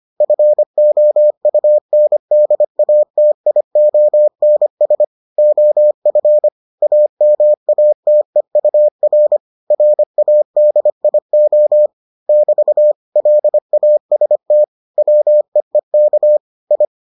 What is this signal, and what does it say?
Morse code, heard as a keyed tone, about 25 wpm, 600 Hz. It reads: FOUNDATIONS OF AMATEUR RADIO <BT> LAST WEEK I